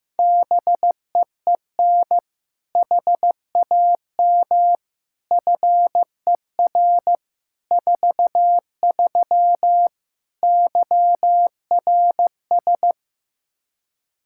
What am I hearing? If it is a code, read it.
Morse code, heard as a keyed tone, about 16 wpm, 700 Hz. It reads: BEEN HAM FER 43 YRS
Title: Practice QSO - Exchange 3 - More Info - 15wpm